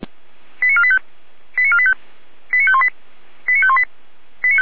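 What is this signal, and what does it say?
Five-tone selective call (ZVEI), heard as quick runs of five tones, tones of about 70 ms. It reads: 89485 89485 89518 89518
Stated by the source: ZVEI five-tone selective call 89361, received by SDR, from the audio.f85360.feuerwehr-orlamuende.org fire brigade server.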